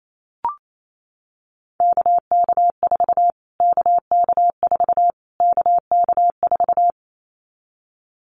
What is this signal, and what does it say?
Morse code, heard as a keyed tone, about 28 wpm, 700 Hz. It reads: KK4 KK4 KK4
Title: Callsign Prefixes for Top 25 DXCC - Repeated 3x - 28wpm